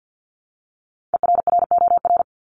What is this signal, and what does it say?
Morse code, heard as a keyed tone, about 48 wpm, 700 Hz. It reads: ERROR
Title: Top 1751-2000 Words 50wpm